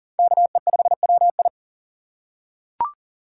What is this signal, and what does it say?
Morse code, heard as a keyed tone, about 37 wpm, 700 Hz. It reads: KE5WI
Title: USA Call Signs - 5 Characters - 6x Character Spacing - 40wpm